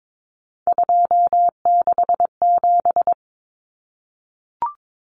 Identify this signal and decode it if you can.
Morse code, heard as a keyed tone, about 22 wpm, 700 Hz. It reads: 267
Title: Numbers - 3-Digit 22wpm